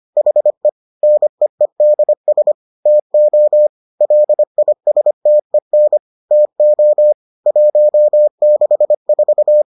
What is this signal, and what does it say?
Morse code, heard as a keyed tone, about 25 wpm, 600 Hz. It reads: HE NEEDS TO LISTEN TO 164